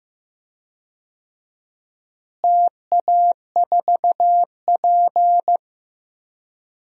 Morse code, heard as a keyed tone, about 15 wpm, 700 Hz. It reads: TA4P
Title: DXCC-100 Call Signs - 4 Characters - Repeated 2x - 15wpm